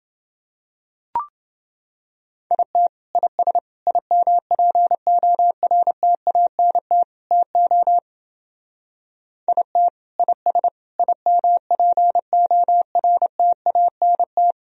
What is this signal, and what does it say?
Morse code, heard as a keyed tone, about 29 wpm, 700 Hz. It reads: IT IS IMPORTANT TO IT IS IMPORTANT